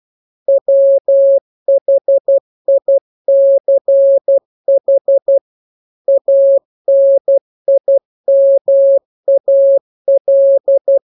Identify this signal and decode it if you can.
Morse code, heard as a keyed tone, about 12 wpm, 550 Hz. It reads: WHICH ANIMAL